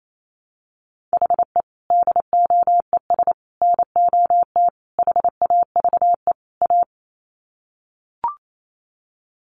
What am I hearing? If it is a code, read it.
Morse code, heard as a keyed tone, about 28 wpm, 700 Hz. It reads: HE DOES NOT HAVE A
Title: Sets of 5 Words Encore 28wpm